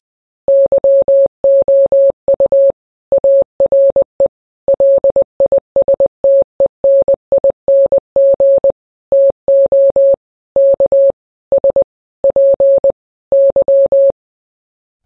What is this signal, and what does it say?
Morse code, heard as a keyed tone, about 20 wpm, 550 Hz. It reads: YOU ARE LISTENING TO K S P Y
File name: radio / Imaginary Stations / KSPY 9395 kHz 29.1.23